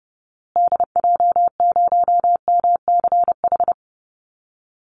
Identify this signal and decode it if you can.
Morse code, heard as a keyed tone, about 30 wpm, 700 Hz. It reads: DJ0MCH
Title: Call Signs Repeated 2x - 30wpm